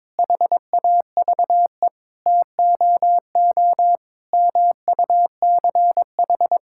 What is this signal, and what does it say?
Morse code, heard as a keyed tone, about 22 wpm, 700 Hz. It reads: HAVE TOO MUCH
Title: Sentences from Top 1000 Words - Review Part 5 - 22wpm